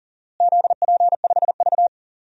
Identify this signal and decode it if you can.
Morse code, heard as a keyed tone, about 40 wpm, 700 Hz. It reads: ZP5V
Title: DXCC-100 Call Signs - 4 Characters - Repeated 3x - 40wpm